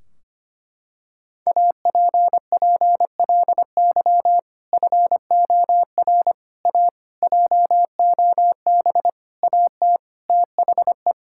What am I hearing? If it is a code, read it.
Morse code, heard as a keyed tone, about 25 wpm, 700 Hz. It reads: APPLY FOR A JOB AT THE
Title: Sentences from Top 1600 Words 25wpm